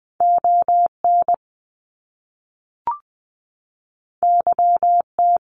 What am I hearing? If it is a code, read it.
Morse code, heard as a keyed tone, about 20 wpm, 700 Hz. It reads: ON YT
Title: Canadian Provinces and Territories 20wpm